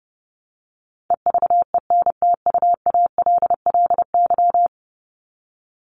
Morse code, heard as a keyed tone, about 30 wpm, 700 Hz. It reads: EVENTUALLY
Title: LY Suffix - Top 100 Words with Base Words 30wpm